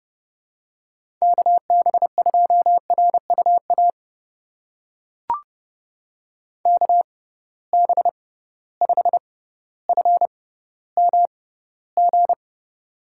Morse code, heard as a keyed tone, about 31 wpm, spaced as for 16 wpm, 700 Hz. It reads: KB2RUA KB5FMG
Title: USA Call Signs - 6 Characters - 6x Character Spacing - 30wpm